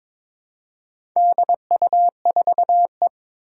Morse code, heard as a keyed tone, about 23 wpm, 700 Hz. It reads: DU4E
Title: Four Challenging Letter-Number Combinations 22wpm